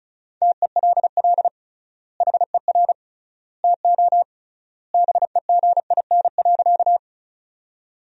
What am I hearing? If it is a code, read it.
Morse code, heard as a keyed tone, about 36 wpm, 700 Hz. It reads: TELL HER TO BEGIN.
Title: 3x Word-Spacing - Sentences from Top 300 Words - 35wpm